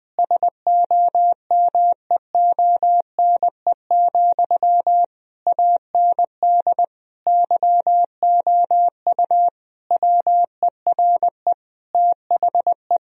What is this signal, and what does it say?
Morse code, heard as a keyed tone, about 20 wpm, 700 Hz. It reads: SOMEONE, AND YOU WERE THE
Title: Sentences from Top 1000 Words - Review Part 4 - 20wpm